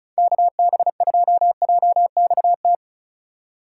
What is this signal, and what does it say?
Morse code, heard as a keyed tone, about 36 wpm, 700 Hz. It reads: KB2JXT
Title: USA Call Signs - 6 Characters - 4x Character Spacing - 35wpm